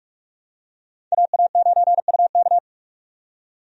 Morse code, heard as a keyed tone, about 43 wpm, 700 Hz. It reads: AA9UK
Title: USA Call Signs - 5 Characters - 45wpm